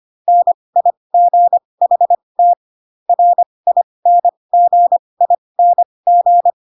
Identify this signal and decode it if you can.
Morse code, heard as a keyed tone, about 25 wpm, spaced as for 20 wpm, 700 Hz. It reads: NIGHT RINGING
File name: Pinocchio / Pinocchio ch6 15 25 ik1hge